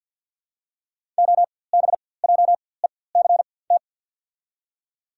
Morse code, heard as a keyed tone, about 54 wpm, spaced as for 24 wpm, 700 Hz. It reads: OBJECT
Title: 4x Character Spacing - Top 500 Words - 50wpm